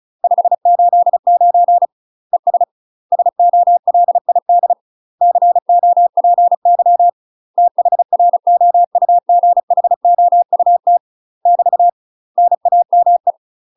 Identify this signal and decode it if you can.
Morse code, heard as a keyed tone, about 36 wpm, 700 Hz. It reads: 589 ES SOLID COPY THROUGHOUT <BT> NAME
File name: UK MASTER QSOs TRAINING / 35WPM / UK MASTER QSOs 0020